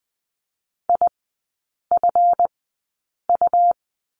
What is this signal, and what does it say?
Morse code, heard as a keyed tone, about 20 wpm, spaced as for 8 wpm, 700 Hz. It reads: IFU
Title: 138 Lesson 23e - Letter f - Three Characters 20wpm